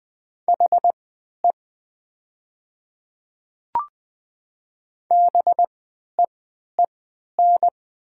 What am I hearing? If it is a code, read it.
Morse code, heard as a keyed tone, about 20 wpm, spaced as for 11 wpm, 690 Hz. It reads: HE BEEN
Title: 3x Character Spacing - Top 100 Words - 20wpm